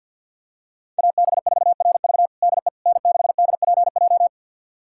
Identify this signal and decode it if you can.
Morse code, heard as a keyed tone, about 51 wpm, 700 Hz. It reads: AB4RV DE N6DPJ
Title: USA Callsign DE Callsign - 5 Characters - 50wpm